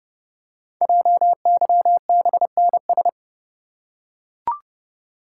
Morse code, heard as a keyed tone, about 30 wpm, 700 Hz. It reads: JYBNS
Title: Code Groups - 5 Letters - 30wpm